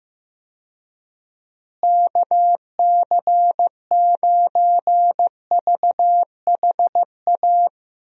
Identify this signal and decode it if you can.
Morse code, heard as a keyed tone, about 15 wpm, 700 Hz. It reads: KC9VHA
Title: USA Call Signs - 6 Characters - Repeated 3x - 15wpm